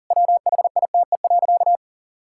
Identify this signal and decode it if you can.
Morse code, heard as a keyed tone, about 40 wpm, 700 Hz. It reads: WHITE.
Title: Pangram Challenge v2 - 40wpm